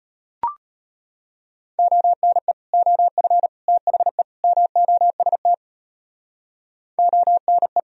Morse code, heard as a keyed tone, about 32 wpm, 700 Hz. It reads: ONE OF THE MOST ONE